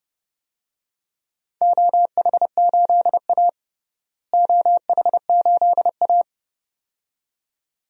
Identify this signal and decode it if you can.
Morse code, heard as a keyed tone, about 30 wpm, 700 Hz. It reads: OH8A OH8A
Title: DXCC-100 Call Signs - 4 Characters - Repeated 2x - 30wpm